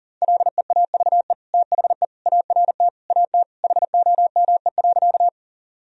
Morse code, heard as a keyed tone, about 40 wpm, 700 Hz. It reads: LEAVE THE ART AT HOME.
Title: Sentences from Top 1000 Words - Review Part 3 - 40wpm